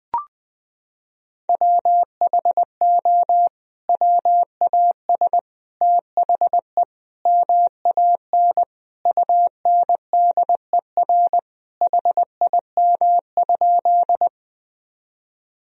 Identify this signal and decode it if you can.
Morse code, heard as a keyed tone, about 21 wpm, 700 Hz. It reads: WHO WAS THE MAN UNDER HIM?
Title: Sentences from Top 1000 Words - Review Part 1 - 20wpm